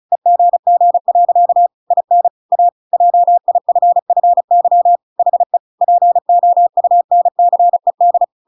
Morse code, heard as a keyed tone, about 35 wpm, 700 Hz. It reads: EGG. IN A JIFFY HE POUNCED